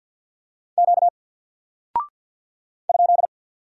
Morse code, heard as a keyed tone, about 48 wpm, 700 Hz. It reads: Q ?
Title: Single Character (Letter:Number:Punctuation:Prosign) Alt 50wpm